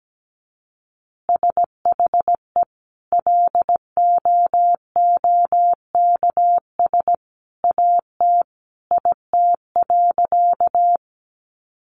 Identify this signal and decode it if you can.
Morse code, heard as a keyed tone, about 17 wpm, 700 Hz. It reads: SHE LOOKS AT IT.
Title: Additional Easy Phrases 17wpm